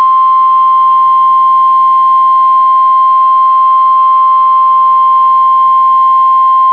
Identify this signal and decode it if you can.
Emergency Alert System (SAME) tones (the 1050 Hz attention tone). It is heard from 0.7 to 5.6 s.